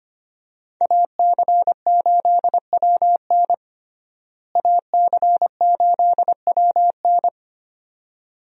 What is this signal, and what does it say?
Morse code, heard as a keyed tone, about 25 wpm, 700 Hz. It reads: AC8WN AC8WN
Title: USA Call Signs - 5 Characters - Repeated 2x - 25wpm